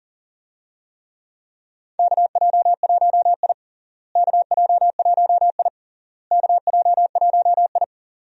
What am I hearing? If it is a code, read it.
Morse code, heard as a keyed tone, about 37 wpm, 700 Hz. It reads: KJ1I KJ1I KJ1I
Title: USA Call Signs - 4 Characters - Repeated 3x - 40wpm